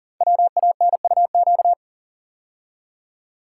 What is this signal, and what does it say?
Morse code, heard as a keyed tone, about 39 wpm, 700 Hz. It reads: WANUQ
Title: Code Groups - 5 Letter:Numbers - 40wpm